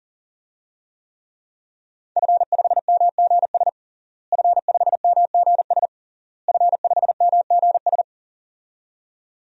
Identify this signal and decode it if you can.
Morse code, heard as a keyed tone, about 40 wpm, 700 Hz. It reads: F5MGS F5MGS F5MGS
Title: DXCC-100 Call Signs - 5 Characters - Repeated 3x - 40wpm